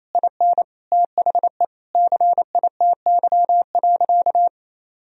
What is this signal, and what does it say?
Morse code, heard as a keyed tone, about 28 wpm, 700 Hz. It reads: IN THE CITY.
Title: Sentences from Top 1000 Words - Review Part 1 - 28wpm